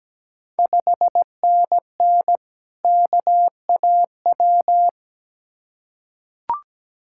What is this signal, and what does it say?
Morse code, heard as a keyed tone, about 17 wpm, 700 Hz. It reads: 5NN KAW
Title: Ontario QSO Party Exchange 17wpm